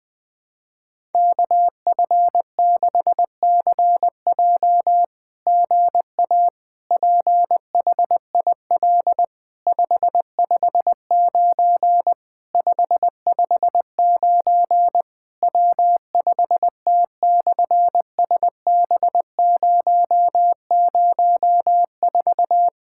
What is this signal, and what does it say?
Morse code, heard as a keyed tone, about 20 wpm, 700 Hz. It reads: KF6CJ GA PHIL 559 559 W5T/SB004
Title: SOTA QSO Practice 20wpm